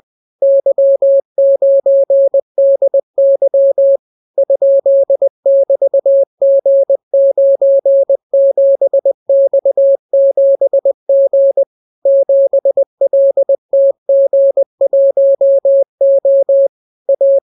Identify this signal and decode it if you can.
Morse code, heard as a keyed tone, about 20 wpm, 550 Hz. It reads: Y9DY ?<BT>G97X7G 7LTG1O A